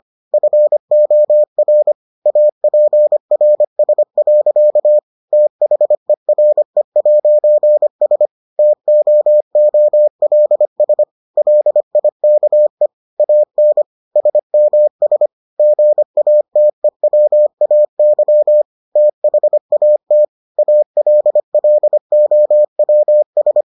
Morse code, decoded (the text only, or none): FOR APRS. THERE'S TOOLS LIKE AN SMS GATEWAY THAT ALLOWS